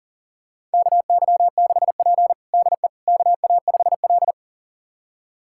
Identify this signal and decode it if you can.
Morse code, heard as a keyed tone, about 40 wpm, 700 Hz. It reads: KY6P DE KA5L
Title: USA Callsign DE Callsign - 4 Characters - 40wpm